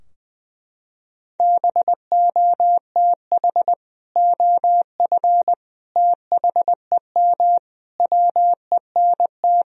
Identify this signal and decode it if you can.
Morse code, heard as a keyed tone, about 20 wpm, 700 Hz. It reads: BOTH OF THEM WENT
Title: Sentences from Top 1000 Words - Review Part 2 - 20wpm